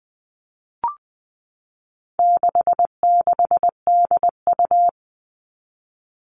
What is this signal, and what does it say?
Morse code, heard as a keyed tone, about 20 wpm, 700 Hz. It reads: 66DU
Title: All Cut Numbers - 4-Digit 20wpm